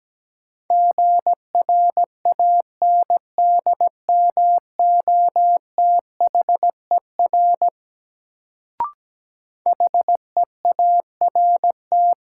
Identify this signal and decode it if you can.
Morse code, heard as a keyed tone, about 17 wpm, 700 Hz. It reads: GRANDMOTHER HEART
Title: Sentences from Top 1200 Words - New Word Drill - 17wpm